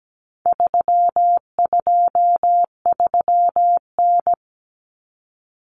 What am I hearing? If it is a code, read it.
Morse code, heard as a keyed tone, about 17 wpm, 700 Hz. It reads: 323N
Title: Common Cut Numbers - 4-Digit 17wpm